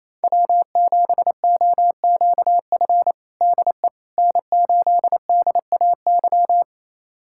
Morse code, heard as a keyed tone, about 28 wpm, 700 Hz. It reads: W7OQF DE N8DAY
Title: USA Callsign DE Callsign - 5 Characters - 28wpm